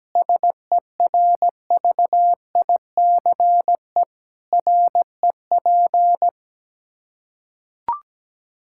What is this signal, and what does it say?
Morse code, heard as a keyed tone, about 17 wpm, 700 Hz. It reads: SERVICE REP
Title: Practice QSO - Exchange 3 - More Info - 17wpm